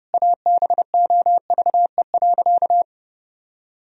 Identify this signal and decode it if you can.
Morse code, heard as a keyed tone, about 30 wpm, 700 Hz. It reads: ABOVE.
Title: Sentences from Top 1000 Words - Review Part 4 - 30wpm